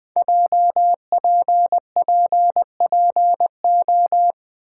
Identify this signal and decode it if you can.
Morse code, heard as a keyed tone, about 20 wpm, 700 Hz. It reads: JPPPO
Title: Code Groups - 5 Letters - 20wpm